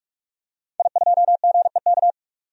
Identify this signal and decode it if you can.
Morse code, heard as a keyed tone, about 41 wpm, 700 Hz. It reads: I2GEK
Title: DXCC-100 Call Signs - 5 Characters - 45wpm